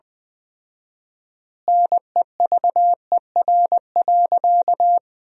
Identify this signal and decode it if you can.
Morse code, heard as a keyed tone, about 20 wpm, 700 Hz. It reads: NEVER.
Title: 4x Word-Spacing - Sentences from Top 300 Words - 20wpm